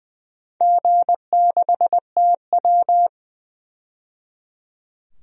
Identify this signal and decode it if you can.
Morse code, heard as a keyed tone, about 21 wpm, 700 Hz. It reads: G6TW
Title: DXCC-100 Call Signs - 4 Characters - 20wpm